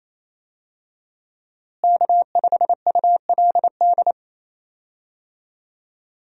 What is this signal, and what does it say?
Morse code, heard as a keyed tone, about 28 wpm, 700 Hz. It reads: K5ULD
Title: USA Call Signs - 5 Characters - Repeated 2x - 28wpm